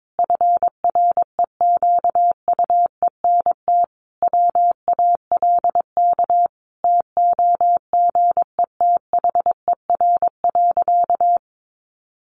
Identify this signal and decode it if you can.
Morse code, heard as a keyed tone, about 22 wpm, 700 Hz. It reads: FREQUENT WALK TOGETHER.
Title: Sentences from Top 1400 Words 22wpm